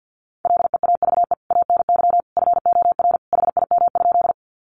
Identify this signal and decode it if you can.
Morse code, heard as a keyed tone, about 49 wpm, 700 Hz. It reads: LEAVE ANY FOR HIM?
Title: Sentences from Top 400 Words 50wpm